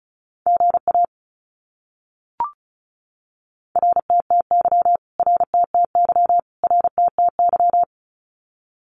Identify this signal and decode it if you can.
Morse code, heard as a keyed tone, about 34 wpm, 700 Hz. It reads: GA RTTY RTTY RTTY
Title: Abbreviations Repeated 3x - 35wpm